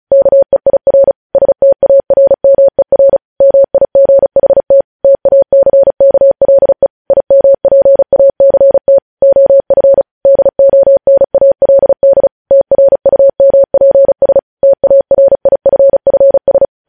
Morse code, heard as a keyed tone, about 36 wpm, 550 Hz. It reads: KEIR STARMER MIGHT TACKLE IMPACT OF DONALD TRUMPS TARIFFS
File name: podcast / News headlines-1x-35x35